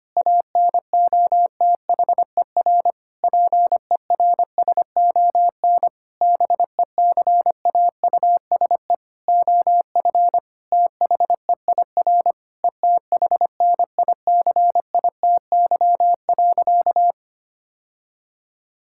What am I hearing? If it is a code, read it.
Morse code, heard as a keyed tone, about 25 wpm, 700 Hz. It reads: ANOTHER PERSON BECAUSE OF THEIR ETHNICITY.
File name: practice / 25wpm / Sentences with 9-Word Phrases - 25wpm